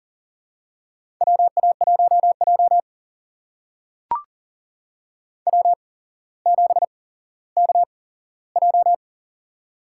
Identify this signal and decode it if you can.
Morse code, heard as a keyed tone, about 44 wpm, spaced as for 17 wpm, 700 Hz. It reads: WA1J W7KJ
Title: USA Call Signs - 4 Characters - 8x Character Spacing - 40wpm